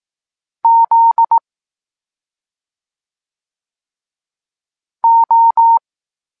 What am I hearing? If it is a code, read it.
Morse code, heard as a keyed tone, about 18 wpm, 920 Hz. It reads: Z O
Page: Tekst 23.3 – Ciąg dalszy doskonalenia odbioru poznanych liter